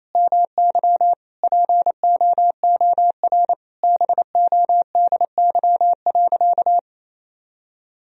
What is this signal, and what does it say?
Morse code, heard as a keyed tone, about 28 wpm, 700 Hz. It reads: MY POOR BODY.
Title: Sentences from Top 1000 Words - Review Part 2 - 28wpm